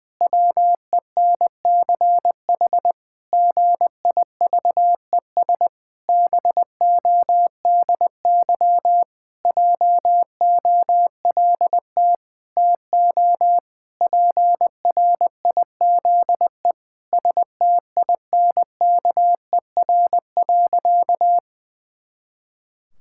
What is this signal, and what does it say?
Morse code, heard as a keyed tone, about 19 wpm, 700 Hz. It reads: WENCH GIVES BODY JOLT TO PRIZE STINKER.
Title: Pangram Challenge v2 - 20wpm